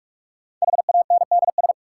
Morse code, heard as a keyed tone, about 46 wpm, 700 Hz. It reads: HANDS
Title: Sentences from Top 1000 Words - Review Part 6 - New Word Drill - 45wpm